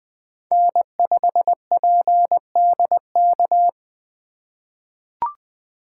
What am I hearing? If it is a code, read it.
Morse code, heard as a keyed tone, about 20 wpm, 700 Hz. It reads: N5PDK
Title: USA Call Signs - 5 Characters - 2x Character Spacing - 20wpm